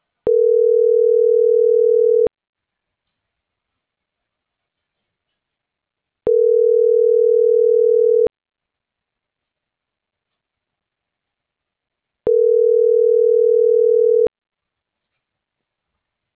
Ringback tone (North America), 440 + 480 Hz.